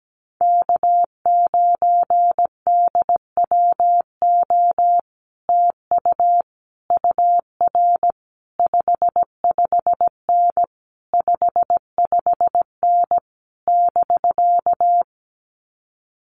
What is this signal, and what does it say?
Morse code, heard as a keyed tone, about 17 wpm, 700 Hz. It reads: K9DWO TU UR 55N 55N <BK>
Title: Common QSO Phrases Encore 17wpm